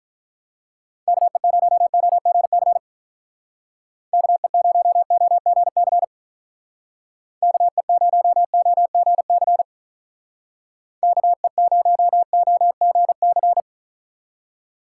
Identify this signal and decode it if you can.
Morse code, heard as a keyed tone, about 43 wpm, 700 Hz. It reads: KE0OGC KE0OGC KE0OGC KE0OGC
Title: Speed Racing - USA 2x3 Call Signs 35wpm